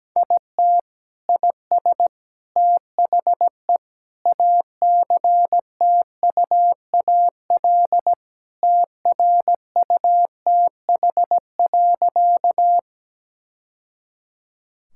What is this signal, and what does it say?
Morse code, heard as a keyed tone, about 17 wpm, 700 Hz. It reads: IT IS THE ACTUAL TRUTH.
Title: Sentences from Top 1000 Words - Review Part 5 - 17wpm